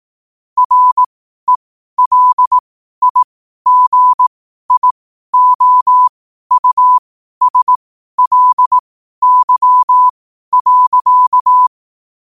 Morse code, heard as a keyed tone, about 18 wpm, spaced as for 12 wpm, 1000 Hz. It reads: RELIGIOUSLY.
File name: qotdfiles / 12wpm